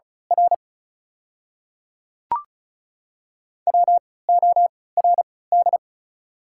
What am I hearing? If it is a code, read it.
Morse code, heard as a keyed tone, about 36 wpm, spaced as for 20 wpm, 690 Hz. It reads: R WORD
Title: 3x Character Spacing - Top 200 Words - 35wpm